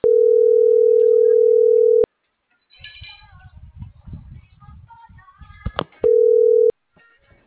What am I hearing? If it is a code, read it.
Ringback tone (North America), 440 + 480 Hz.